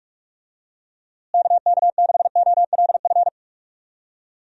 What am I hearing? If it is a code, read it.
Morse code, heard as a keyed tone, about 46 wpm, 700 Hz. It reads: KK6OLF
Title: USA Call Signs - 6 Characters - Repeated 3x - 45wpm